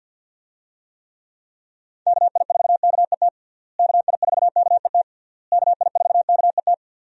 Morse code, heard as a keyed tone, about 51 wpm, 700 Hz. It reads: KI4KET KI4KET KI4KET
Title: USA Call Signs - 6 Characters - Repeated 3x - 50wpm